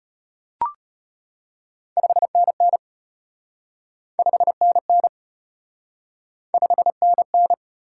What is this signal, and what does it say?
Morse code, heard as a keyed tone, about 33 wpm, 700 Hz. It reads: 5NN 5NN 5NN